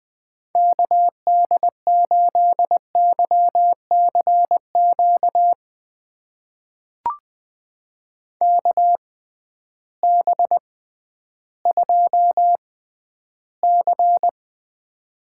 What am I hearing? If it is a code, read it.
Morse code, heard as a keyed tone, about 20 wpm, spaced as for 12 wpm, 700 Hz. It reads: KD8YCQ KB2C
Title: USA Call Signs - 6 Characters - 6x Character Spacing - 20wpm